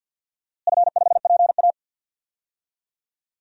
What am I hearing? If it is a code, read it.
Morse code, heard as a keyed tone, about 51 wpm, 700 Hz. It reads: F5PA
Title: DXCC-100 Call Signs - 4 Characters - Repeated 2x - 50wpm